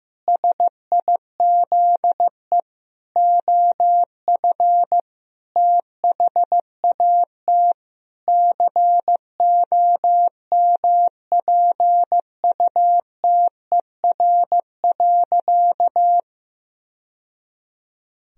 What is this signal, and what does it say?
Morse code, heard as a keyed tone, about 15 wpm, 700 Hz. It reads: SIZE OF THAT COMPUTER.
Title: Sentences from Top 1000 Words - Review Part 3 - 15wpm